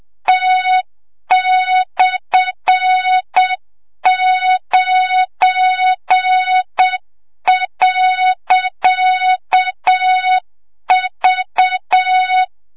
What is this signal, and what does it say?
Morse code, heard as a keyed tone, about 7 wpm, 3700 Hz, with noise. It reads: T/9.V
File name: Morze-HA2MN / hang / 25 GYAKORLAT